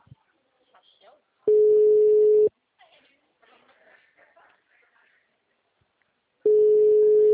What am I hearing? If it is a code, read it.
Ringback tone (Europe, 425 Hz), with speech.
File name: out- -2013- - - .18